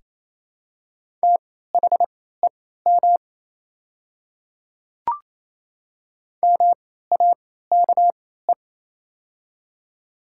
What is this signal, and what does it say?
Morse code, heard as a keyed tone, about 27 wpm, spaced as for 16 wpm, 700 Hz. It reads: THEM MAKE
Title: 3x Character Spacing - Top 300 Words - 28wpm